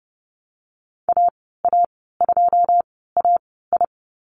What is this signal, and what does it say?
Morse code, heard as a keyed tone, about 30 wpm, spaced as for 17 wpm, 700 Hz. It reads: AA2AI
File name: Lesson 14g - Number 2 - Call Signs 30wpm